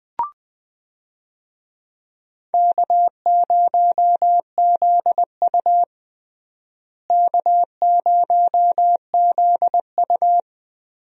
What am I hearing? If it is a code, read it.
Morse code, heard as a keyed tone, about 21 wpm, 700 Hz. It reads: K0ZU K0ZU
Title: USA Call Signs - 4 Characters - Repeated 2x - 20wpm